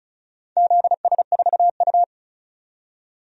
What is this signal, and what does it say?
Morse code, heard as a keyed tone, about 36 wpm, 700 Hz. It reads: ZS4U
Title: DXCC-100 Call Signs - 4 Characters - 35wpm